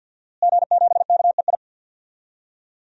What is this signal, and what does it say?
Morse code, heard as a keyed tone, about 52 wpm, 700 Hz. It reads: G7KEI
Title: DXCC-100 Call Signs - 5 Characters - Repeated 3x - 50wpm